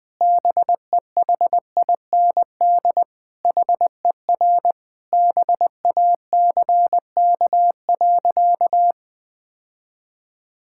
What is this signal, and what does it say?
Morse code, heard as a keyed tone, about 20 wpm, 700 Hz. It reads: BEHIND HER BACK.
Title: Sentences from Top 1000 Words - Review Part 6 - 20wpm